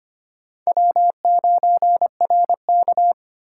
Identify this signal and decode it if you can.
Morse code, heard as a keyed tone, about 32 wpm, 700 Hz. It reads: W9RK